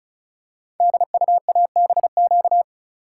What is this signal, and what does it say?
Morse code, heard as a keyed tone, about 36 wpm, 700 Hz. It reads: DUABQ
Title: Code Groups - 5 Letters - 35wpm